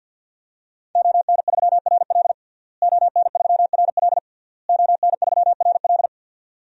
Morse code, heard as a keyed tone, about 52 wpm, 700 Hz. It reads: ON3RL ON3RL ON3RL
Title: DXCC-100 Call Signs - 5 Characters - Repeated 3x - 50wpm